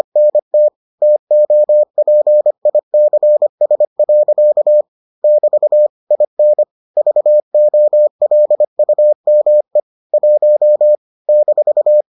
Morse code, heard as a keyed tone, about 25 wpm, 600 Hz. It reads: NT TOPICS. <BT> IN VOLUME 1 -